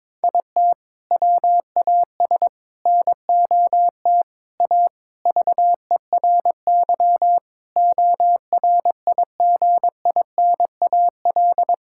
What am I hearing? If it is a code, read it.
Morse code, heard as a keyed tone, about 22 wpm, 700 Hz. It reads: IT WAS NOT A VERY ORIGINAL
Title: Sentences from Top 1000 Words - Review Part 6 - 22wpm